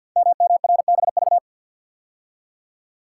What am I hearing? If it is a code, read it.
Morse code, heard as a keyed tone, about 48 wpm, 700 Hz. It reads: MMRBV
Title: Code Groups - 5 Letters - 50wpm